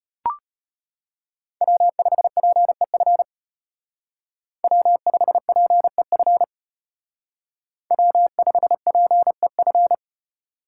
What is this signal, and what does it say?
Morse code, heard as a keyed tone, about 32 wpm, 700 Hz. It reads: W5PEF W5PEF W5PEF